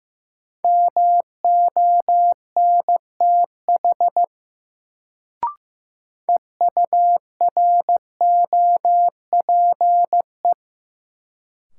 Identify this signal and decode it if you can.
Morse code, heard as a keyed tone, about 15 wpm, 700 Hz. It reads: MONTH EUROPE
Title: Sets of 5 Words Encore - New Word Drill - 15wpm